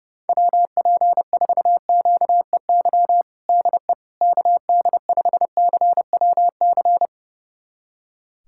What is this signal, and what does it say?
Morse code, heard as a keyed tone, about 30 wpm, 700 Hz. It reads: WP4QEY DE KD5CWC
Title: USA Callsign DE Callsign 30wpm